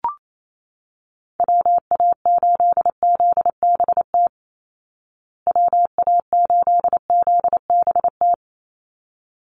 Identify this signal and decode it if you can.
Morse code, heard as a keyed tone, about 28 wpm, 700 Hz. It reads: WA8ZBT WA8ZBT
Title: Call Signs Repeated 3x - 28wpm